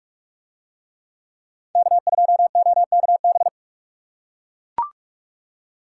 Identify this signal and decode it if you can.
Morse code, heard as a keyed tone, about 39 wpm, 700 Hz. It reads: K2OKB